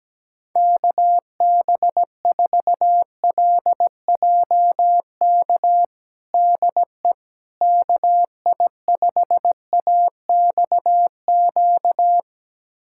Morse code, heard as a keyed tone, about 17 wpm, 700 Hz. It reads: KB4LJK DE KI5AXQ
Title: USA Callsign DE Callsign - 6 Characters - 17wpm